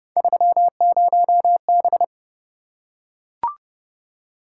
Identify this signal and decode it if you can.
Morse code, heard as a keyed tone, about 29 wpm, 700 Hz. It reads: 30B